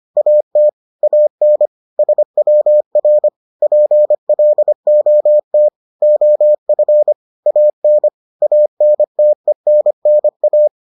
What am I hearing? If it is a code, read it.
Morse code, heard as a keyed tone, about 25 wpm, 600 Hz. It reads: AT AN SWR PLOT OF AN ANTENNA